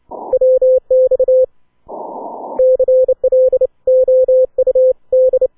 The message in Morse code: WX CLOUD